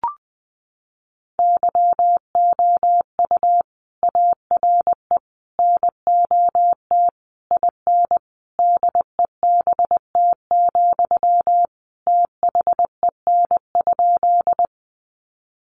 Morse code, heard as a keyed tone, about 20 wpm, 700 Hz. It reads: YOU ARE NOT IN DEBT, THEN?
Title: Sentences from Top 1300 Words 20wpm